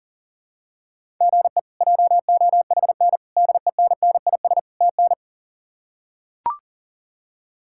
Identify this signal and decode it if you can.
Morse code, heard as a keyed tone, about 40 wpm, 700 Hz. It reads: GE JOHN DENNIS TN
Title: K1USN SST QSO Practice - 40wpm